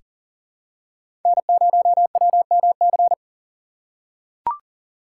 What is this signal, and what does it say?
Morse code, heard as a keyed tone, about 40 wpm, 700 Hz. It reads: N0WMC
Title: USA Call Signs - 5 Characters - 6x Character Spacing - 40wpm